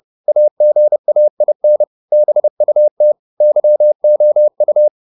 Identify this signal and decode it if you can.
Morse code, heard as a keyed tone, about 30 wpm, 600 Hz. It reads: AGAIN BUT YOU